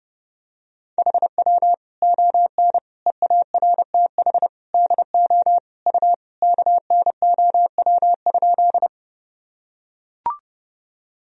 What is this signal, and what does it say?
Morse code, heard as a keyed tone, about 32 wpm, 700 Hz. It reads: HW ON EARTH DO U KNOW?
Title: Sentences from Top 200 Words and Abbreviations 30wpm